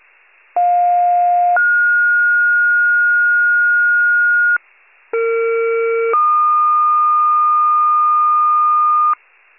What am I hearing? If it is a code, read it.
Two-tone page, heard as a one-second tone, then a three-second tone, about 710 then 1430 Hz, about 460 then 1220 Hz, with noise. It is heard from 0.6 to 9.1 s.
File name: pager / 46 FIRE / wcalert